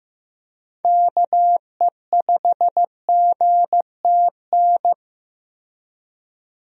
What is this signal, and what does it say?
Morse code, heard as a keyed tone, about 15 wpm, 700 Hz. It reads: KE5GTN
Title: USA Call Signs - 6 Characters - Repeated 2x - 15wpm